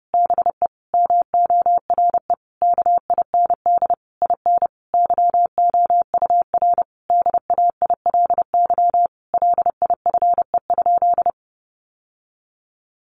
Morse code, heard as a keyed tone, about 30 wpm, 700 Hz. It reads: BE MORE KIND IN YOUR DAILY LIFE?
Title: Sentences from Top 1100 Words 30wpm